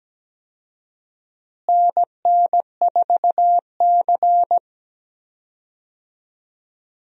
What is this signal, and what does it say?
Morse code, heard as a keyed tone, about 17 wpm, 700 Hz. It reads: NN4C
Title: USA Call Signs - 4 Characters - Repeated 3x - 17wpm